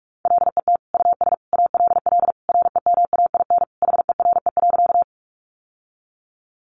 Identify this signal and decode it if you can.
Morse code, heard as a keyed tone, about 44 wpm, 700 Hz. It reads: LET US ALL REMAIN HERE.
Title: Sentences from Top 1200 Words 45wpm